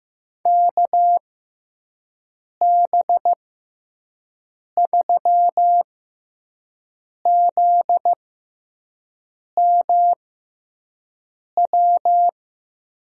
Morse code, heard as a keyed tone, about 15 wpm, spaced as for 5 wpm, 700 Hz. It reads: KB3ZMW
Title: USA Call Signs - 6 Characters - 6x Character Spacing - 15wpm